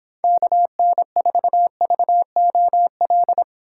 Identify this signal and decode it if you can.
Morse code, heard as a keyed tone, about 26 wpm, 700 Hz. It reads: KN4VOL